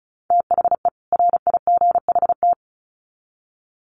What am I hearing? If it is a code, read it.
Morse code, heard as a keyed tone, about 34 wpm, 700 Hz. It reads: THE RIGHT
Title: Sets of 2 Words 35wpm